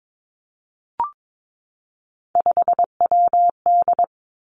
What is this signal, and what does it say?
Morse code, heard as a keyed tone, about 22 wpm, 700 Hz. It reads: 5WD
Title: Three Challenging Letter-Number Combinations 22wpm